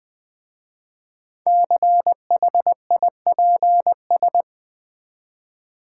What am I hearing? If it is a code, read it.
Morse code, heard as a keyed tone, about 20 wpm, 700 Hz. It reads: CHIPS
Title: Easy Words with Hints - 20wpm